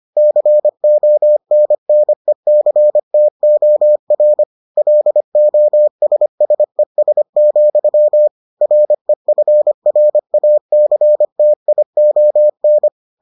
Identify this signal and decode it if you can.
Morse code, heard as a keyed tone, about 25 wpm, 600 Hz. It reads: CONNECTOR LOSSES, REFRACTION